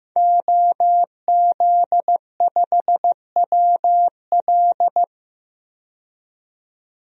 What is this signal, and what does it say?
Morse code, heard as a keyed tone, about 15 wpm, 700 Hz. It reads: OZ5WL
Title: DXCC-100 Call Signs - 5 Characters - 15wpm